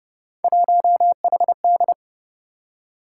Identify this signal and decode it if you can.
Morse code, heard as a keyed tone, about 31 wpm, 700 Hz. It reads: 1HD
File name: Speed-Racing - Three Letter-Number Combinations - 25wpm